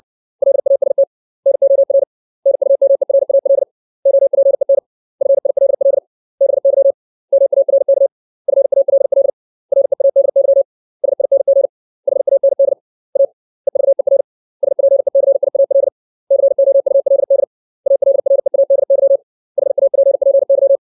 Morse code, heard as a keyed tone, about 60 wpm, 550 Hz. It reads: LAST NOR NUMERAL OPEN FILL BY MARK FALL READY SING HAND A EVER SPECIAL COULD ALREADY HAPPY
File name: 632 words / 632 Words 60WPM